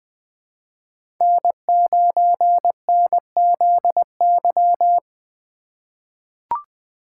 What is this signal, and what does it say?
Morse code, heard as a keyed tone, about 20 wpm, 700 Hz. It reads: N9NZY
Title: USA Call Signs - 5 Characters - 2x Character Spacing - 20wpm